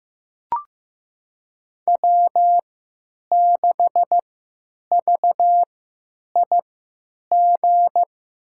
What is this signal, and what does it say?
Morse code, heard as a keyed tone, about 15 wpm, spaced as for 9 wpm, 700 Hz. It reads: W6VIG
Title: USA Call Signs - 5 Characters - 3x Character Spacing - 15wpm